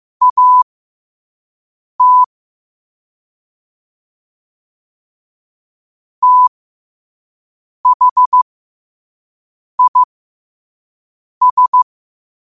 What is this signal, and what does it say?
Morse code, heard as a keyed tone, about 15 wpm, spaced as for 5 wpm, 1000 Hz. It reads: AT THIS
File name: qotdfiles / 5wpm